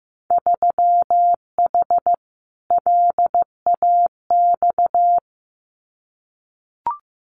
Morse code, heard as a keyed tone, about 15 wpm, 700 Hz. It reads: 3H LAX
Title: Winter Field Day Exchanges 15wpm